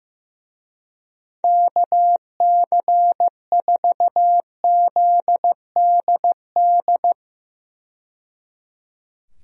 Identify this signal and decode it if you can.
Morse code, heard as a keyed tone, about 15 wpm, 700 Hz. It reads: KC4ZDD
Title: USA Call Signs - 6 Characters - 15wpm